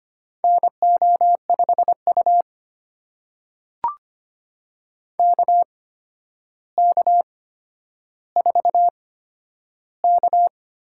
Morse code, heard as a keyed tone, about 25 wpm, spaced as for 11 wpm, 700 Hz. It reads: NO5U KK4K
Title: USA Call Signs - 4 Characters - 8x Character Spacing - 25wpm